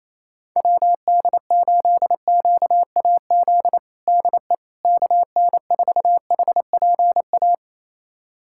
Morse code, heard as a keyed tone, about 28 wpm, 700 Hz. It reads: WD8QAZ DE KN4HPA
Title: USA Callsign DE Callsign - 6 Characters - 28wpm